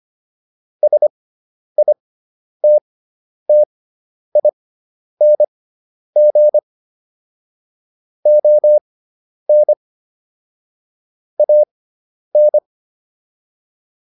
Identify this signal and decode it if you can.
Morse code, heard as a keyed tone, about 25 wpm, spaced as for 10 wpm, 600 Hz. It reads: SITTING ON AN